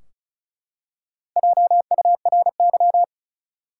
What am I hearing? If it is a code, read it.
Morse code, heard as a keyed tone, about 36 wpm, 700 Hz. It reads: JURY
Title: Four Letter Words - 35wpm